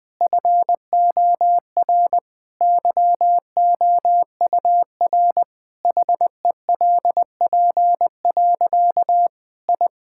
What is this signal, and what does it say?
Morse code, heard as a keyed tone, about 20 wpm, 700 Hz. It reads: FOR YOUR HELP. I
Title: Difficult Sentences with Hints - 20wpm